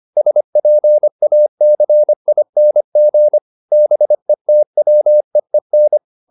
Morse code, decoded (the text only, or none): SPACING BETWEEN